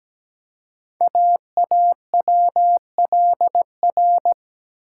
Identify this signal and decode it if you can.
Morse code, heard as a keyed tone, about 17 wpm, 700 Hz. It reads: AAWLR
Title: Code Groups - 5 Letters - 17wpm